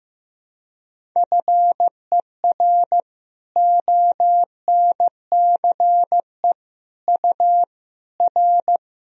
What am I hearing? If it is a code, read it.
Morse code, heard as a keyed tone, about 15 wpm, 700 Hz. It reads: FER ONCE U R
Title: Sentences from Top 300 Words and Abbreviations 15wpm